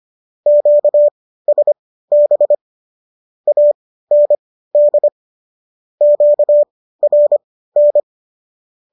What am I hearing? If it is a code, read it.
Morse code, heard as a keyed tone, about 25 wpm, spaced as for 15 wpm, 590 Hz. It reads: QSB AND QRN